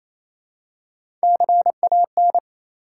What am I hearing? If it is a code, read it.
Morse code, heard as a keyed tone, about 27 wpm, 700 Hz. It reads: CAN
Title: Top 180 QSO Elements 28wpm